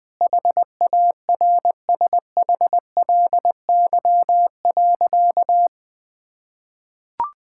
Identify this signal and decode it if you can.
Morse code, heard as a keyed tone, about 21 wpm, 700 Hz. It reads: HARSHLY.
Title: Sentences with 6-Word Phrases - Spoken Beginning - 20wpm